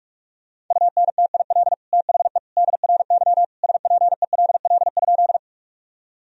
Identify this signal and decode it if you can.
Morse code, heard as a keyed tone, about 46 wpm, 700 Hz. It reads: UNTIL THE DRY SPELL?
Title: Sentences from Top 1000 Words - Review Part 3 - 45wpm